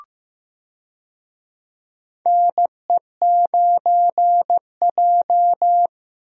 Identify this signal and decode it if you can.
Morse code, heard as a keyed tone, about 15 wpm, 700 Hz. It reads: NE9J
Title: USA Call Signs - 4 Characters - Repeated 3x - 15wpm